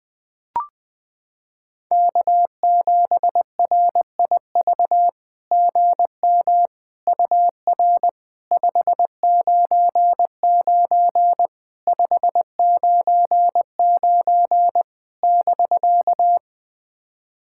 Morse code, heard as a keyed tone, about 21 wpm, 700 Hz. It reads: K7RIV GM UR 599 599 <BK>
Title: SOTA QSO Practice 20wpm